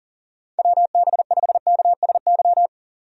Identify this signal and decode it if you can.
Morse code, heard as a keyed tone, about 37 wpm, 700 Hz. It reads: WB5KSY
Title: USA Call Signs - 6 Characters - 40wpm